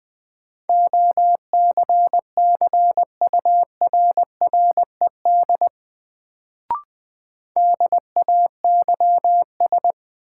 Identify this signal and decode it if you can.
Morse code, heard as a keyed tone, about 20 wpm, 700 Hz. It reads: OCCURRED DAYS
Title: Sets of 5 Words Encore - New Word Drill - 20wpm